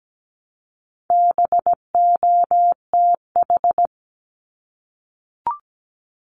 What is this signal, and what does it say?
Morse code, heard as a keyed tone, about 17 wpm, 700 Hz. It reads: BOTH
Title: Top 500 Words Repeated 17wpm